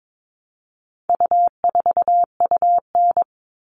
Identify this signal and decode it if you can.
Morse code, heard as a keyed tone, about 22 wpm, 700 Hz. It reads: U4UN
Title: All Cut Numbers - 4-Digit 22wpm